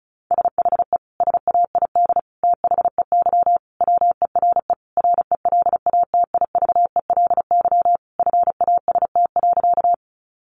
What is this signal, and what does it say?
Morse code, heard as a keyed tone, about 34 wpm, 700 Hz. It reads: SHE SAID THEY WERE RELATIVELY FAST.
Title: Sentences from Top 1100 Words 35wpm